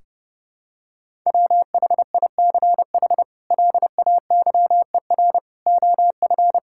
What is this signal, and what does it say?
Morse code, heard as a keyed tone, about 30 wpm, 700 Hz. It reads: WHICH LAYER OF
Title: Sentences from Top 1600 Words 30wpm